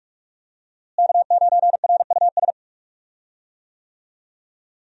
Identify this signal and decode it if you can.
Morse code, heard as a keyed tone, about 42 wpm, 700 Hz. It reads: K9RUS
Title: USA Call Signs - 5 Characters - Repeated 3x - 45wpm